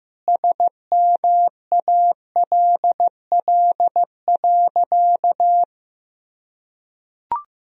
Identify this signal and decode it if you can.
Morse code, heard as a keyed tone, about 16 wpm, 700 Hz. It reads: SMALL.
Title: Sentences from Top 1000 Words - Review Part 1 - 15wpm